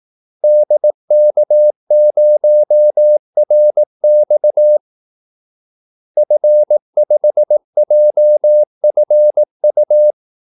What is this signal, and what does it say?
Morse code, heard as a keyed tone, about 18 wpm, 600 Hz. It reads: DK0RX F5JFU
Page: Kaldesignaler 18 wpm